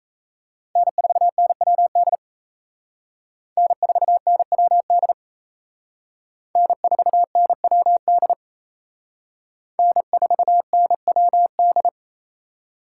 Morse code, heard as a keyed tone, about 35 wpm, 700 Hz. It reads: N4NWD N4NWD N4NWD N4NWD